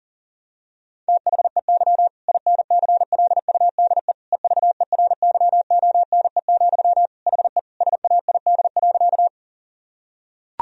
Morse code, heard as a keyed tone, about 40 wpm, 700 Hz. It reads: THEY INCLUDE EVERYONE, HE SAID.
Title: Sentences from Top 1000 Words - Review Part 3 - 40wpm